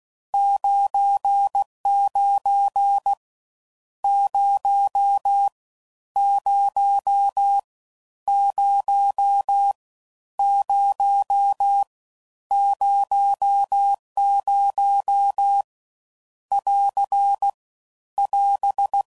Morse code, heard as a keyed tone, about 16 wpm, 800 Hz. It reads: 99 0 0 0 0 00 <AR> <AS>
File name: Mixed / Misc / Numbs